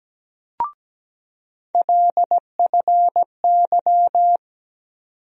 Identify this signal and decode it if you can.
Morse code, heard as a keyed tone, about 17 wpm, 700 Hz. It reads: LFY
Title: USA Call Sign Postfixes - 3 Letters 17wpm